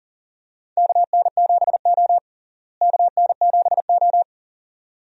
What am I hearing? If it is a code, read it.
Morse code, heard as a keyed tone, about 40 wpm, 700 Hz. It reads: KN7O KN7O
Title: USA Call Signs - 4 Characters - Repeated 2x - 40wpm